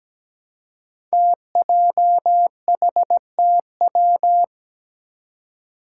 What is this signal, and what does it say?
Morse code, heard as a keyed tone, about 17 wpm, 700 Hz. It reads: TJHTW
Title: Five Challenging Letter-Number Combinations 17wpm